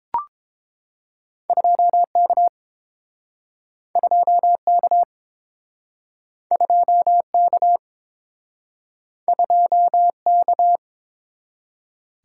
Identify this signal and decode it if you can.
Morse code, heard as a keyed tone, about 30 wpm, 700 Hz. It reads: 2K 2K 2K 2K